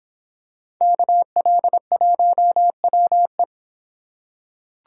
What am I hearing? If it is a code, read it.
Morse code, heard as a keyed tone, about 26 wpm, 700 Hz. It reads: KL1WE